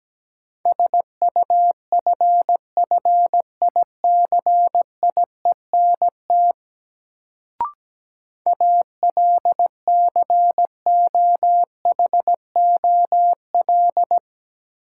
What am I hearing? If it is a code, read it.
Morse code, heard as a keyed tone, about 17 wpm, 700 Hz. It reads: SUFFICIENT ALCOHOL
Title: Sentences from Top 1400 Words - New Word Drill - 17wpm